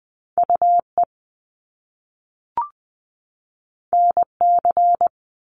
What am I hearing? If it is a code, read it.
Morse code, heard as a keyed tone, about 20 wpm, 700 Hz. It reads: UE NC
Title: Top Two Letter Combinations in Words 20wpm